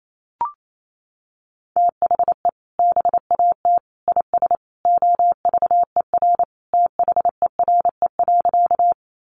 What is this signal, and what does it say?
Morse code, heard as a keyed tone, about 28 wpm, 700 Hz. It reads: THE BAT IS OVER THERE.
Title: Sentences from Top 1500 Words 28wpm